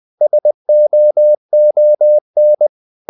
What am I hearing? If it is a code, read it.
Morse code, heard as a keyed tone, about 20 wpm, 600 Hz. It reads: SOON
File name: the old man and the sea / the old man and the sea part 1 20wpm